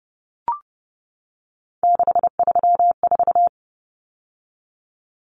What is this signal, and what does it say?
Morse code, heard as a keyed tone, about 30 wpm, 700 Hz. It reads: 634
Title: Common Cut Numbers - 3-Digit 30wpm